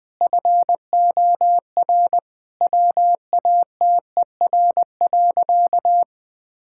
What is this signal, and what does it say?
Morse code, heard as a keyed tone, about 21 wpm, 700 Hz. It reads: FOR WATER.
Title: Sentences from Top 1000 Words - Review Part 5 - 20wpm